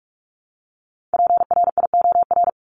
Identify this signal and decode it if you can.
Morse code, heard as a keyed tone, about 44 wpm, 700 Hz. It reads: PRIOR
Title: Top 1501-1750 Words 45wpm